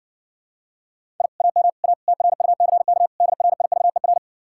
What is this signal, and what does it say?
Morse code, heard as a keyed tone, about 60 wpm, 700 Hz. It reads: I AM A TRUCK DRIVER